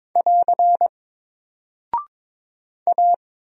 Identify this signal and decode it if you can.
Morse code, heard as a keyed tone, about 22 wpm, 700 Hz. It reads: <AR> A
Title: Single Character (Letter:Number:Punctuation:Prosign) 22wpm